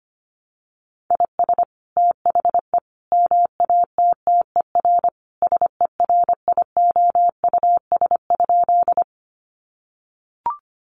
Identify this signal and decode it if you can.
Morse code, heard as a keyed tone, about 25 wpm, 700 Hz. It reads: IS THE MATTER SERIOUS?
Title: Sentences from Top 1100 Words 25wpm